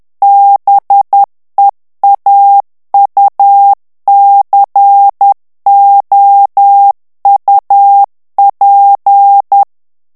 Morse code, heard as a keyed tone, about 10 wpm, 800 Hz. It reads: BEAUCOUP
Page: Français - texte en clair 1 - vitesse : 10 mots minute